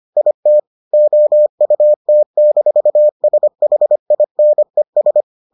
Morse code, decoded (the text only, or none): IT OUT-SHINES